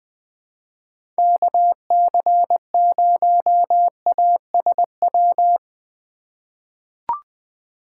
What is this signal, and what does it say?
Morse code, heard as a keyed tone, about 20 wpm, 700 Hz. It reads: KC0ASW
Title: USA Call Signs - 8x Character Spacing - 20wpm